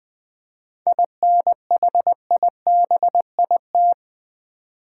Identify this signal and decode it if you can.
Morse code, heard as a keyed tone, about 20 wpm, 700 Hz. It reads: INHIBIT
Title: 7 Letter Words - First 3 Letters Given - 20wpm